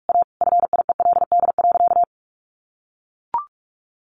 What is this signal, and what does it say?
Morse code, heard as a keyed tone, about 44 wpm, 700 Hz. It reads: A FIELD.
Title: Sentences from Top 1000 Words 45wpm